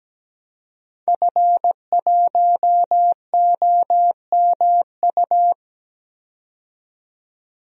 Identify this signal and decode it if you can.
Morse code, heard as a keyed tone, about 17 wpm, 700 Hz. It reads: F1OMU
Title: DXCC-100 Call Signs - 5 Characters - 17wpm